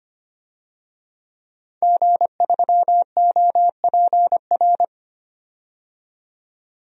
Morse code, heard as a keyed tone, about 25 wpm, 700 Hz. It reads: G3OPR
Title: DXCC-100 Call Signs - 5 Characters - 25wpm